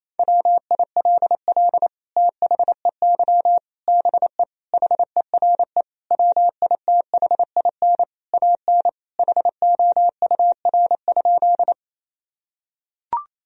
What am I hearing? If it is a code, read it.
Morse code, heard as a keyed tone, about 28 wpm, 700 Hz. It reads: WILL THEY BE HERE WITHIN AN HOUR?
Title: Sentences from Top 1000 Words - Review Part 4 - 28wpm